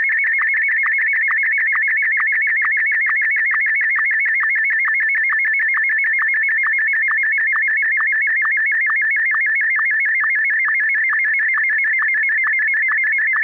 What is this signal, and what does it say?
SSTV (slow-scan television), Martin 1.